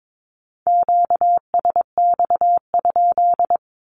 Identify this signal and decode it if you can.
Morse code, heard as a keyed tone, about 22 wpm, 700 Hz. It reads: QSX?
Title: Q-Codes 22wpm